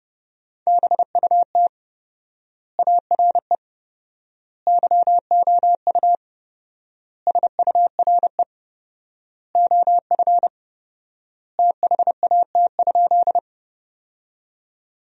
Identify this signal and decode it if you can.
Morse code, heard as a keyed tone, about 30 wpm, 700 Hz. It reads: BUT ARE YOU SURE OF THAT?
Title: 4x Word-Spacing - Sentences from Top 300 Words - 30wpm